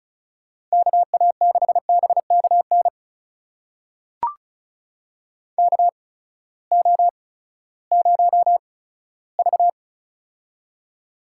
Morse code, heard as a keyed tone, about 36 wpm, spaced as for 18 wpm, 700 Hz. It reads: KA6BKN KO0V
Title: USA Call Signs - 8x Character Spacing - 35wpm